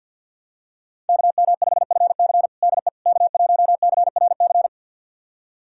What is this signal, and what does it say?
Morse code, heard as a keyed tone, about 51 wpm, 700 Hz. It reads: KM5FC DE K1CRC
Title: USA Callsign DE Callsign - 5 Characters - 50wpm